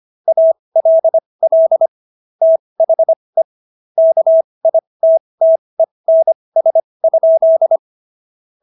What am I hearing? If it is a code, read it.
Morse code, heard as a keyed tone, about 25 wpm, spaced as for 20 wpm, 650 Hz. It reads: ALL THE KITTENS?